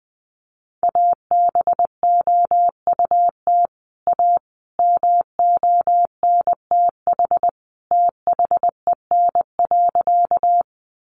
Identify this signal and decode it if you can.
Morse code, heard as a keyed tone, about 20 wpm, 700 Hz. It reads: ABOUT A MONTH THEN.
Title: Sentences from Top 800 Words 20wpm